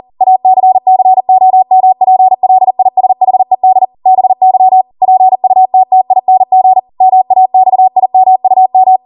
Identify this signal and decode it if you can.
Morse code, heard as a keyed tone, about 40 wpm, 750 Hz. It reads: ACCOMPLISHED BY PUTTING MAXIMUM